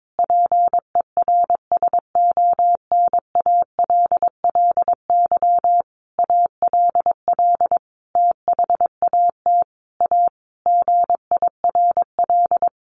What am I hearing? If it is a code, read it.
Morse code, heard as a keyed tone, about 22 wpm, 700 Hz. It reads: PERSONALLY ALL THAT A GIRL
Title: Sentences from Top 1400 Words 22wpm